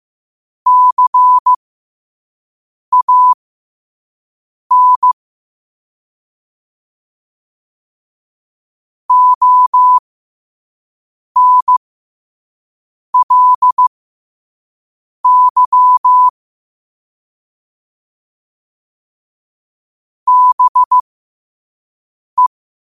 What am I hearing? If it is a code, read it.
Morse code, heard as a keyed tone, about 15 wpm, spaced as for 5 wpm, 1000 Hz. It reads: CAN ONLY BE
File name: qotdfiles / 5wpm